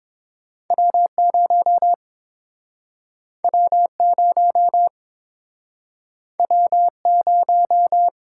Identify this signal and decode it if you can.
Morse code, heard as a keyed tone, about 30 wpm, 700 Hz. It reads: W0 W0 W0